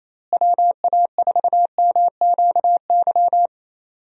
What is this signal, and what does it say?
Morse code, heard as a keyed tone, about 27 wpm, 700 Hz. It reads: WA4MQY
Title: USA Call Signs - 6 Characters - Repeated 3x - 28wpm